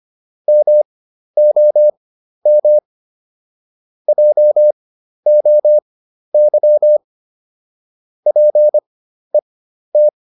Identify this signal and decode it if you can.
Morse code, heard as a keyed tone, about 25 wpm, spaced as for 12 wpm, 600 Hz. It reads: MOM JOY PET